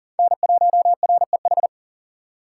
Morse code, heard as a keyed tone, about 51 wpm, 700 Hz. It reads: N1REH